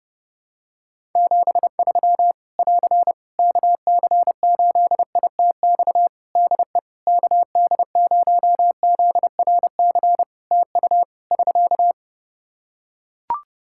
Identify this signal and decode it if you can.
Morse code, heard as a keyed tone, about 30 wpm, 700 Hz. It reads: 73 <AR> KC8ITX DE KD0ZRC TU <SK>
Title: Practice QSO - Exchange 4 - Ending - 30wpm